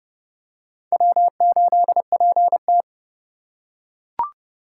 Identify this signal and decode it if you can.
Morse code, heard as a keyed tone, about 30 wpm, 700 Hz. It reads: W8PT
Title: USA Call Signs - 2x Character Spacing - 30wpm